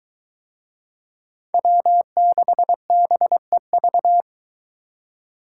Morse code, heard as a keyed tone, about 22 wpm, 700 Hz. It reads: W6BEV